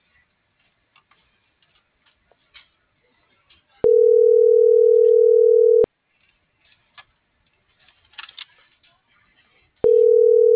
Ringback tone (North America), 440 + 480 Hz.